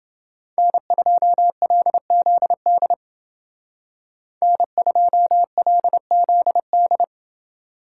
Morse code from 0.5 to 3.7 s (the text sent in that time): N2LZD